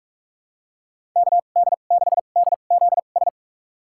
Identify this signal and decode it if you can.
Morse code, heard as a keyed tone, about 43 wpm, spaced as for 32 wpm, 700 Hz. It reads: KD6DZS
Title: USA Call Signs - 6 Characters - 2x Character Spacing - 45wpm